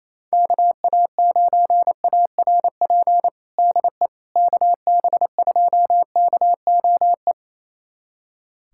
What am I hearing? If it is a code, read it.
Morse code, heard as a keyed tone, about 27 wpm, 700 Hz. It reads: KA9ARP DE KB2KOE
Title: USA Callsign DE Callsign - 6 Characters - 28wpm